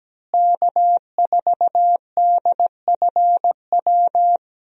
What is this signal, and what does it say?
Morse code, heard as a keyed tone, about 17 wpm, 700 Hz. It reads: K4DFW
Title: USA Call Signs - 5 Characters - Repeated 2x - 17wpm